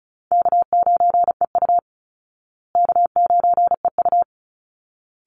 Morse code, heard as a keyed tone, about 34 wpm, 700 Hz. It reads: K9EU K9EU
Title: Easy Call Signs Repeated 2x - 35wpm